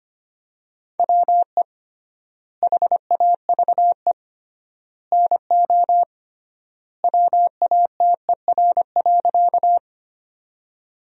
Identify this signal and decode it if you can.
Morse code, heard as a keyed tone, about 25 wpm, 700 Hz. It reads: WE HAVE NO WATER.
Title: 3x Word-Spacing - Sentences from Top 100 Words - 25wpm